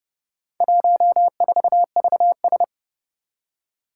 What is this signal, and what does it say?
Morse code, heard as a keyed tone, about 29 wpm, 700 Hz. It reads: 14VS
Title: Four Challenging Letter-Number Combinations 30wpm